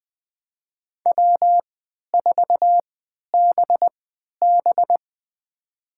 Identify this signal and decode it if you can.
Morse code, heard as a keyed tone, about 20 wpm, spaced as for 11 wpm, 700 Hz. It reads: W4BB
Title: USA Call Signs - 4 Characters - 3x Character Spacing - 20wpm